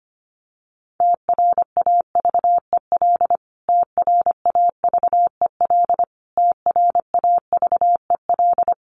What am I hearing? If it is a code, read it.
Morse code, heard as a keyed tone, about 25 wpm, 700 Hz. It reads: TRAVEL TRAVEL TRAVEL
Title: Top 500 Words Repeated 25wpm